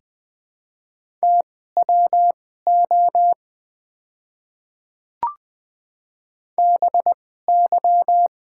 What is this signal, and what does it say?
Morse code, heard as a keyed tone, about 20 wpm, spaced as for 15 wpm, 690 Hz. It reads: TWO BY
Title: 2x Character Spacing - Top 100 Words - 20wpm